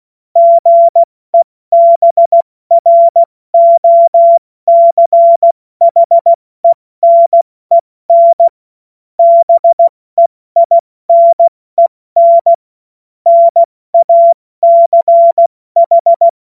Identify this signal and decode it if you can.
Morse code, heard as a keyed tone, about 16 wpm, 680 Hz. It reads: GEBROCHENEN BEINEN NACH
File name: Morsekurs / Witze / Witz 21. September 2021, 70 F80